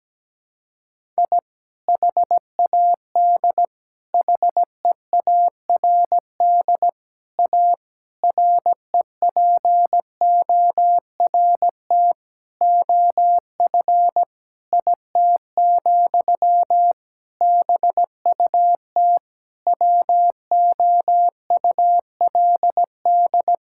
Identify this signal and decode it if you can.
Morse code, heard as a keyed tone, about 17 wpm, 700 Hz. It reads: I HAD HEARD A REPORT OF IT, BUT WOULD
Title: Sentences from Top 1000 Words - Review Part 6 - 17wpm